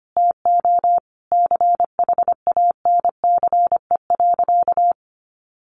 Morse code, heard as a keyed tone, about 25 wpm, 700 Hz. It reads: TO CHANCE.
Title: Sentences from Top 1100 Words 25wpm